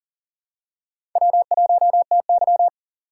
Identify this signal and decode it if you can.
Morse code, heard as a keyed tone, about 40 wpm, 700 Hz. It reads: W1TY
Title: USA Call Signs - 4 Characters - 4x Character Spacing - 40wpm